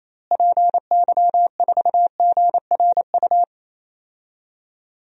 Morse code, heard as a keyed tone, about 28 wpm, 700 Hz. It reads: PY4GRU
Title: DXCC-100 Call Signs - 6 Characters - Repeated 2x - 28wpm